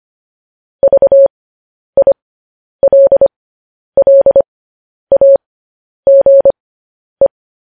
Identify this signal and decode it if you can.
Morse code, heard as a keyed tone, about 25 wpm, spaced as for 10 wpm, 550 Hz. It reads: VILLAGE
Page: news headlines in morse code at 10 wpm